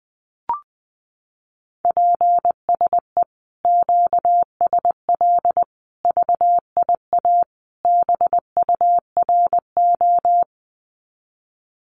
Morse code, heard as a keyed tone, about 20 wpm, 700 Hz. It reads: PSE QSL VIA BURO
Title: Common QSO Phrases Encore 20wpm